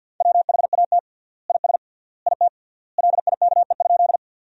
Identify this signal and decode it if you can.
Morse code, heard as a keyed tone, about 50 wpm, 700 Hz. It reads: WHAT IS IT LIKE?
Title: 3x Word-Spacing - Sentences from Top 100 Words - 50wpm